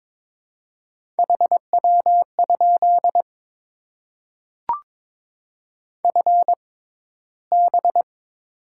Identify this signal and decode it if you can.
Morse code, heard as a keyed tone, about 22 wpm, spaced as for 14 wpm, 700 Hz. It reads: HW? FB
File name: Top 20 QSO Elements - 6x Character Spacing - 22wpm